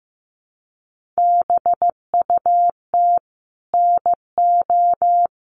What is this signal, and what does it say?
Morse code, heard as a keyed tone, about 15 wpm, 700 Hz. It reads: BUT NO
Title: Sentences from Top 1400 Words 15wpm